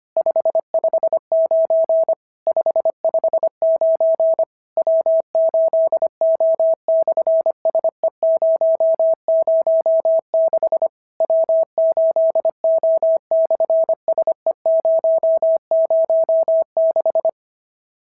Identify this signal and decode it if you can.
Morse code, heard as a keyed tone, about 26 wpm, 650 Hz. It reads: 559 559 W8O/SE006 W8O/SE006
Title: SOTA QSO Practice 25wpm